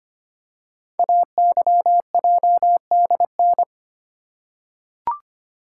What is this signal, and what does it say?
Morse code, heard as a keyed tone, about 25 wpm, 700 Hz. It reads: AYJDN
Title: Code Groups - 5 Letters - 25wpm